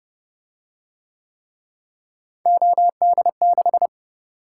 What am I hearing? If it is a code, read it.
Morse code, heard as a keyed tone, about 29 wpm, 700 Hz. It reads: OD6